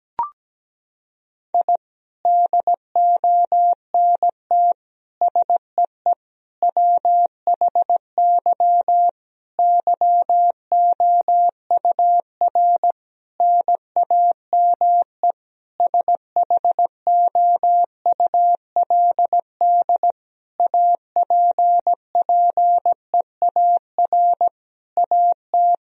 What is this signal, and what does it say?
Morse code, heard as a keyed tone, about 17 wpm, 700 Hz. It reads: I DONT SEE WHY YOUR NAME SHOULD APPEAR AT
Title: Sentences from Top 1000 Words - Review Part 2 - 17wpm